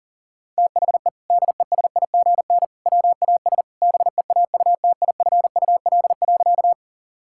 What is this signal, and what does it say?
Morse code, heard as a keyed tone, about 40 wpm, 700 Hz. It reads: THE DESIGN WAS BEAUTIFUL.
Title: Sentences from Top 1000 Words - Review Part 5 - 40wpm